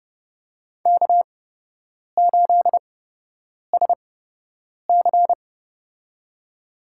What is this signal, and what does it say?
Morse code, heard as a keyed tone, about 29 wpm, spaced as for 8 wpm, 700 Hz. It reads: K8SC
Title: USA Call Signs - 4 Characters - 8x Character Spacing - 30wpm